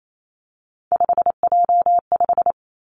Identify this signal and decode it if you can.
Morse code, heard as a keyed tone, about 28 wpm, 690 Hz. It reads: 5J5
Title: Three Challenging Letter-Number Combinations 28wpm